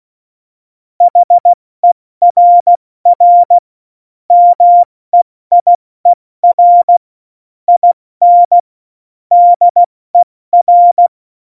Morse code, heard as a keyed tone, about 16 wpm, 700 Hz. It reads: HERR MEIER IN DER
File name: Morsekurs / Witze / Witz 02 Juni 2020, 70 F80